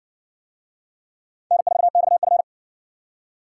Morse code, heard as a keyed tone, about 53 wpm, 700 Hz. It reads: N4XF